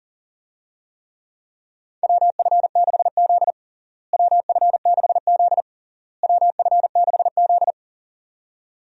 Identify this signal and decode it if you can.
Morse code, heard as a keyed tone, about 40 wpm, 700 Hz. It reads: WF6Z WF6Z WF6Z
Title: USA Call Signs - 4 Characters - Repeated 3x - 40wpm